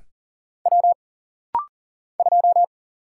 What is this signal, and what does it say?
Morse code, heard as a keyed tone, about 40 wpm, 700 Hz. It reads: W 2
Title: Single Letter-Number - Rapid-Fire 40wpm